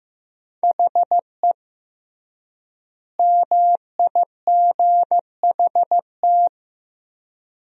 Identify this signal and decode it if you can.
Morse code, heard as a keyed tone, about 15 wpm, 700 Hz. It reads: HE MIGHT
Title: 3x Word-Spacing - Sentences from Top 300 Words - 15wpm